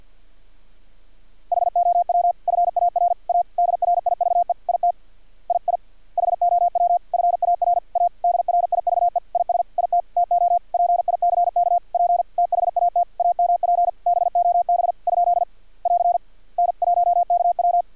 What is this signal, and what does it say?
Morse code, heard as a keyed tone, about 50 wpm, 700 Hz. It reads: HOW FAR A DRIVE IT II HOW FAR A DRIVE IS IT TO PICK P THAT AMP BOB ? <AR> N1KW
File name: archived / CRSnet